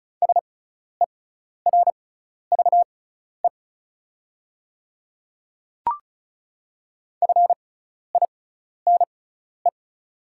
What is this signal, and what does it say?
Morse code, heard as a keyed tone, about 34 wpm, spaced as for 12 wpm, 690 Hz. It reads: SERVE FINE
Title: 6x Character Spacing - Top 500 Words - 35wpm